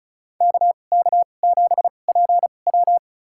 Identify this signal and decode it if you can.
Morse code, heard as a keyed tone, about 36 wpm, spaced as for 25 wpm, 700 Hz. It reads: KK7PW
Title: USA Call Signs - 5 Characters - 2x Character Spacing - 35wpm